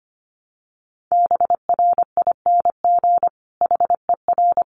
Morse code, heard as a keyed tone, about 25 wpm, 700 Hz. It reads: BRING HER
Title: Sentences from Top 1000 Words 25wpm